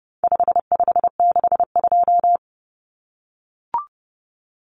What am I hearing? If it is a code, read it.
Morse code, heard as a keyed tone, about 30 wpm, 700 Hz. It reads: 5562
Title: Common Cut Numbers - 4-Digit 30wpm